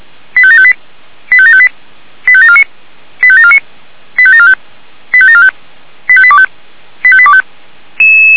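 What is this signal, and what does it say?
Five-tone selective call (ZVEI), heard as quick runs of five tones, tones of about 70 ms. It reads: 85758 85758 85749 85749 85745 85745 85825 85825
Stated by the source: ZVEI five-tone selective call received via SDR.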